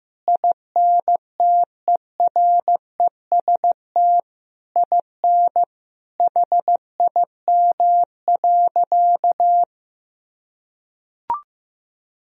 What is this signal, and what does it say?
Morse code, heard as a keyed tone, about 15 wpm, 700 Hz. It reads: INTEREST IN HIM.
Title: Sentences from Top 1000 Words - Review Part 3 - 15wpm